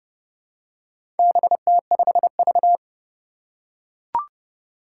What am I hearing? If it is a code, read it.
Morse code, heard as a keyed tone, about 30 wpm, 700 Hz. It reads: BT5V
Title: Four Challenging Letter-Number Combinations 30wpm